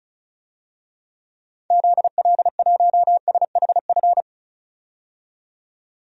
Morse code, heard as a keyed tone, about 36 wpm, 690 Hz. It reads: ZL1SHF
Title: DXCC-100 Call Signs - 6 Characters - 35wpm